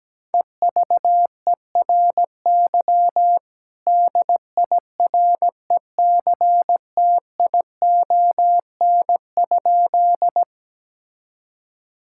Morse code, heard as a keyed tone, about 17 wpm, 700 Hz. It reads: EVERY DIRECTION?
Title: Sentences from Top 1000 Words - Review Part 5 - 17wpm